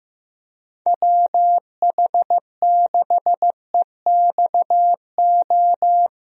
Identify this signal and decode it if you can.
Morse code, heard as a keyed tone, about 15 wpm, 700 Hz. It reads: WH6EXO
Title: USA Call Signs - 6 Characters - Repeated 3x - 15wpm